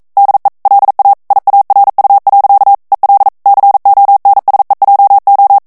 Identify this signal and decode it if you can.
Morse code, heard as a keyed tone, about 42 wpm, 800 Hz. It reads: DE LA IARU. EL CONSEJO
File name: morse mp3 / ESPAGNOL TEXTE EN CLAIR 1 VITESSE 40 MOTS MINUTES